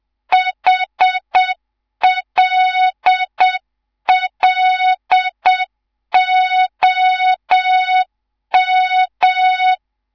Morse code, heard as a keyed tone, about 7 wpm, 3700 Hz. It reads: HLLOM
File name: Morze-HA2MN / hang / 11 GYAKORLAT